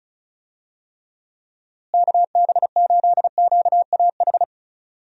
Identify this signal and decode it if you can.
Morse code, heard as a keyed tone, about 36 wpm, 700 Hz. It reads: KB8QAH
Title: USA Call Signs - 6 Characters - Repeated 2x - 35wpm